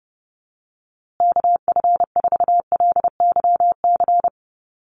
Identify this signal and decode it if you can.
Morse code, heard as a keyed tone, about 29 wpm, 700 Hz. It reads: KF4LYC
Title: USA Call Signs Repeated 2x 30wpm